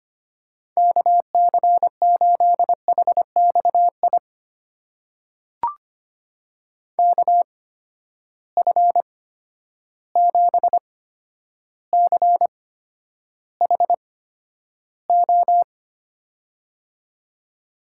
Morse code, heard as a keyed tone, about 25 wpm, spaced as for 11 wpm, 700 Hz. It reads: KC8HXI KF7CHO
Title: USA Call Signs - 6 Characters - 8x Character Spacing - 25wpm